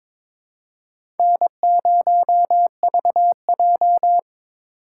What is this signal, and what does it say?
Morse code, heard as a keyed tone, about 22 wpm, 700 Hz. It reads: N0VJ
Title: USA Call Signs - 3x Character Spacing - 22wpm